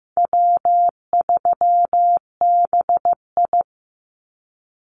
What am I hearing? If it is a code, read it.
Morse code, heard as a keyed tone, about 15 wpm, 700 Hz. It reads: W3BI
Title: Easy Call Signs Repeated 2x - 15wpm